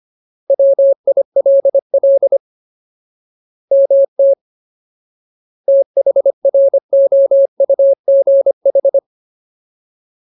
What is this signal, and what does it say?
Morse code, heard as a keyed tone, about 25 wpm, 550 Hz. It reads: WILL MT THROUGH
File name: Ultimate Word List / 25WPM Ultimate